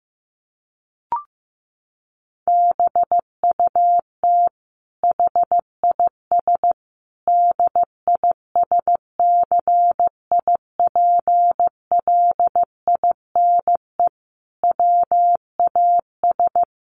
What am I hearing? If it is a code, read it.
Morse code, heard as a keyed tone, about 15 wpm, 700 Hz. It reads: BUT HIS DISCIPLINE WAS
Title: Sentences from Top 1400 Words 15wpm